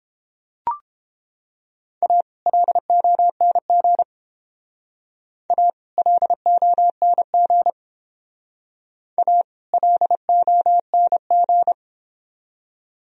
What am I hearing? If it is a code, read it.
Morse code, heard as a keyed tone, about 30 wpm, 700 Hz. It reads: A LONG A LONG A LONG